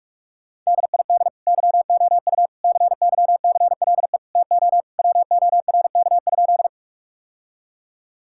Morse code, heard as a keyed tone, about 46 wpm, 700 Hz. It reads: DID YOU CYCLE TO WORK?
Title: Sentences from Top 1000 Words - Review Part 6 - 45wpm